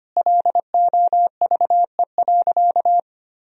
Morse code, heard as a keyed tone, about 25 wpm, 700 Hz. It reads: LOVE.
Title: Sentences from Top 400 Words and Abbreviations 25wpm